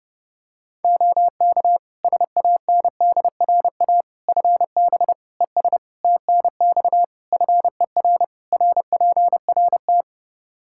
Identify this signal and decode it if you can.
Morse code, heard as a keyed tone, about 30 wpm, 700 Hz. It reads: OK SANDRA FB ES TNX FER RPRT
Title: Practice QSO - Exchange 2 - Info - 30wpm